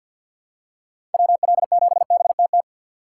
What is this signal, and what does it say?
Morse code, heard as a keyed tone, about 51 wpm, 700 Hz. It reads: WL7BTT
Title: USA Call Signs - 3x Character Spacing - 50wpm